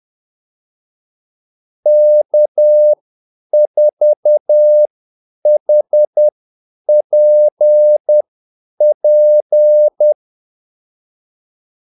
Morse code, heard as a keyed tone, about 10 wpm, spaced as for 8 wpm, 600 Hz. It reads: K4HPP
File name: Callsigns-10-8wpm